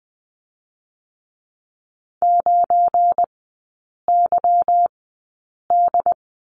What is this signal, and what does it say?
Morse code, heard as a keyed tone, about 20 wpm, spaced as for 8 wpm, 690 Hz. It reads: 9YD
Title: 172 Lesson 28e - Number 9 - Three Characters 20wpm